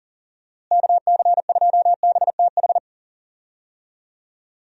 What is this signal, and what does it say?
Morse code, heard as a keyed tone, about 40 wpm, 700 Hz. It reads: KC2BTH
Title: USA Call Signs - 6 Characters - Repeated 2x - 40wpm